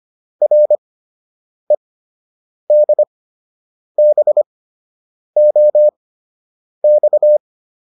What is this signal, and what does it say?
Morse code, heard as a keyed tone, about 25 wpm, spaced as for 8 wpm, 600 Hz. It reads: REDBOX